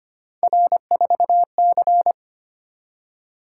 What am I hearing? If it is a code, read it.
Morse code, heard as a keyed tone, about 25 wpm, 700 Hz. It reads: R4C
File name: Three Letter-Number Combinations - 25wpm